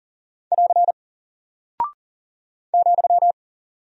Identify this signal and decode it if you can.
Morse code, heard as a keyed tone, about 39 wpm, 700 Hz. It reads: <AR> ,
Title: Single Character (Letter:Number:Punctuation:Prosign) Alt 40wpm